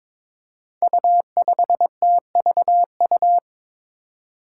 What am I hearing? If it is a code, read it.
Morse code, heard as a keyed tone, about 22 wpm, 700 Hz. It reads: U5TVU
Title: Five Challenging Letter-Number Combinations 22wpm